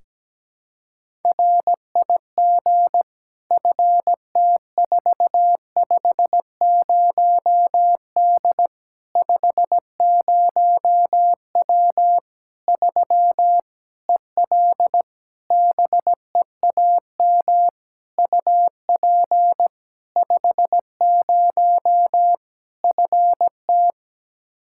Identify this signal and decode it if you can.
Morse code, heard as a keyed tone, about 17 wpm, 700 Hz. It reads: RIG FT450D 50W 3 EL BEAM UP 50 FT
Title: K1USN SST QSO Practice 17wpm